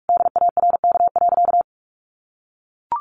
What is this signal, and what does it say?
Morse code, heard as a keyed tone, about 44 wpm, 700 Hz. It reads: DARK.
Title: Sentences from Top 1100 Words 45wpm